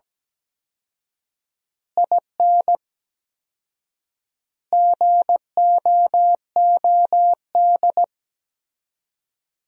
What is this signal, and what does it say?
Morse code, heard as a keyed tone, about 17 wpm, 700 Hz. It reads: IN GOOD
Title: 4x Word-Spacing - Sentences from Top 200 Words - 17wpm